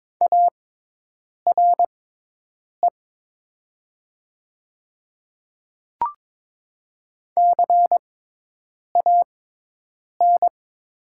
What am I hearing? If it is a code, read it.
Morse code, heard as a keyed tone, about 22 wpm, spaced as for 8 wpm, 700 Hz. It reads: ARE CAN
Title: 6x Character Spacing - Top 200 Words - 22wpm